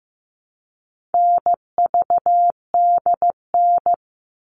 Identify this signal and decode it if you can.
Morse code, heard as a keyed tone, about 15 wpm, 700 Hz. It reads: NVDN
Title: All Cut Numbers - 4-Digit 15wpm